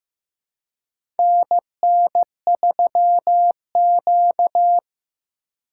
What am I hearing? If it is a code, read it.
Morse code, heard as a keyed tone, about 16 wpm, 700 Hz. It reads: NN3Q
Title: USA Call Signs - 4x Character Spacing - 15wpm